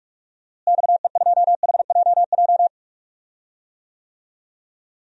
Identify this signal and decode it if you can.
Morse code, heard as a keyed tone, about 45 wpm, 700 Hz. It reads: XE2HJJ
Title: DXCC-100 Call Signs - 6 Characters - Repeated 3x - 45wpm